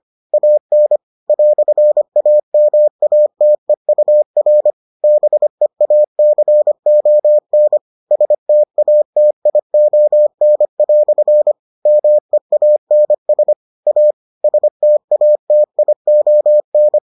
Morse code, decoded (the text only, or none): AN "AMATEUR BEACON STATION" MEANS A STATION